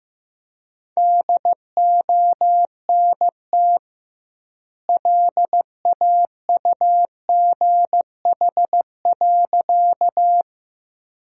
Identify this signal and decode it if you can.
Morse code, heard as a keyed tone, about 15 wpm, 700 Hz. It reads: DONT LAUGH.
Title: 2x Word-Spacing - Sentences from Top 500 Words - 15wpm